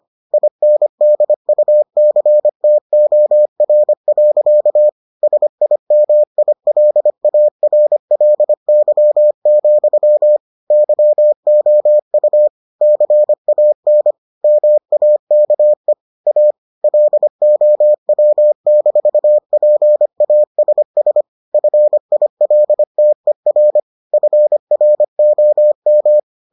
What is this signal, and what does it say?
Morse code, heard as a keyed tone, about 25 wpm, 600 Hz. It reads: INDUCTOR. SIMILARLY, YOU CAN MAKE A LOW-PASS FILTER FROM